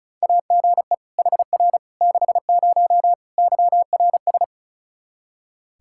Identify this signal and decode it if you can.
Morse code, heard as a keyed tone, about 34 wpm, 700 Hz. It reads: AGE HR 60 YRS
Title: Practice QSO - Exchange 3 - More Info - 35wpm